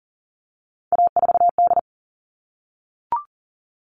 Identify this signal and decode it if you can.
Morse code, heard as a keyed tone, about 40 wpm, 700 Hz. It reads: A4D
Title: All Cut Numbers - 3-Digit 40wpm